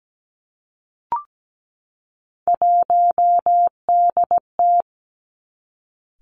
Morse code, heard as a keyed tone, about 17 wpm, 700 Hz. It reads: 1DT
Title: Three Challenging Letter-Number Combinations 17wpm